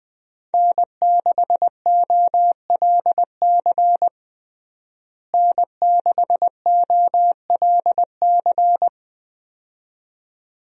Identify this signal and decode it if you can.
Morse code, heard as a keyed tone, about 21 wpm, 700 Hz. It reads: N6OLC N6OLC
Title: USA Call Signs - 5 Characters - Repeated 2x - 20wpm